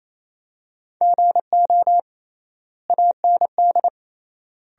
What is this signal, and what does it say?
Morse code, heard as a keyed tone, about 28 wpm, 700 Hz. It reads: GO AND
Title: 3x Word-Spacing - Sentences from Top 200 Words - 28wpm